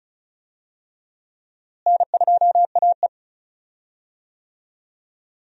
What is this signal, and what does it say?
Morse code, heard as a keyed tone, about 36 wpm, 700 Hz. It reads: N2AE
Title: USA Call Signs - 4 Characters - Repeated 3x - 35wpm